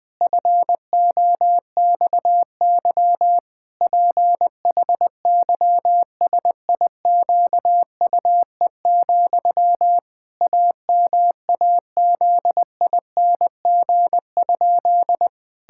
Morse code, heard as a keyed tone, about 21 wpm, 700 Hz. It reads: FOXY PHYSIQUE, AMAZING?
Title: Pangram Challenge v2 - 20wpm